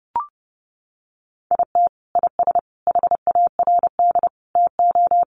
Morse code, heard as a keyed tone, about 30 wpm, 700 Hz. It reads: IT IS HARD TO
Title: Sets of 5 Words Encore2 30wpm